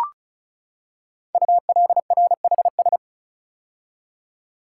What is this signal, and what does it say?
Morse code, heard as a keyed tone, about 36 wpm, 700 Hz. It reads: ULRHS
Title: Code Groups - 5 Letters - 35wpm